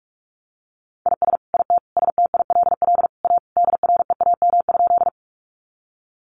Morse code, heard as a keyed tone, about 44 wpm, 700 Hz. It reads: IS IT STILL A DREAM?
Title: Sentences from Top 1500 Words 45wpm